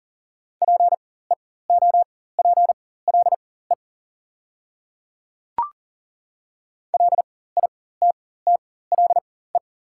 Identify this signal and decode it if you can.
Morse code, heard as a keyed tone, about 40 wpm, spaced as for 19 wpm, 690 Hz. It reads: PEOPLE LITTLE
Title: 4x Character Spacing - Top 200 Words - 40wpm